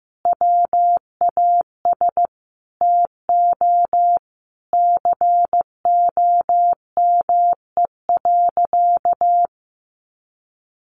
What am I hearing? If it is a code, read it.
Morse code, heard as a keyed tone, about 15 wpm, 700 Hz. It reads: WAS TO COME.
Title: Sentences from Top 700 Words 15wpm